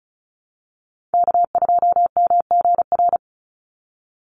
Morse code, heard as a keyed tone, about 34 wpm, 700 Hz. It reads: K2MGR
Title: Call Signs Repeated 3x - 35wpm